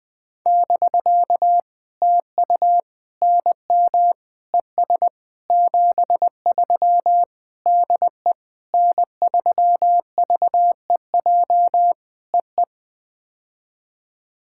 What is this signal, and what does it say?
Morse code, heard as a keyed tone, about 19 wpm, 700 Hz. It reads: <BK> TU NM ES 73 DE N3VEJ EE
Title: POTA QSO Practice - 20wpm